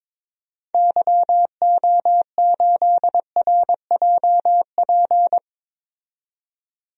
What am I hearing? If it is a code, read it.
Morse code, heard as a keyed tone, about 21 wpm, 700 Hz. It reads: YO8RJP
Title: DXCC-100 Call Signs - 6 Characters - 22wpm